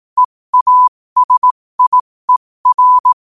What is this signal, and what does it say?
Morse code, heard as a keyed tone, about 18 wpm, spaced as for 15 wpm, 1000 Hz. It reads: EASIER
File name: qotdfiles / 15wpm